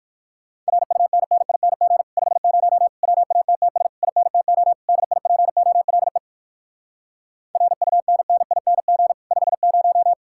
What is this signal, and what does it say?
Morse code, heard as a keyed tone, about 57 wpm, 700 Hz. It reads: RUNNING 50 WATTS INTO DIPOLE RUNNING 50